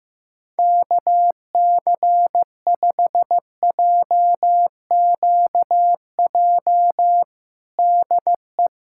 Morse code, heard as a keyed tone, about 15 wpm, 700 Hz. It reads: KC5JQJ DE
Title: USA Callsign DE Callsign - 6 Characters - 15wpm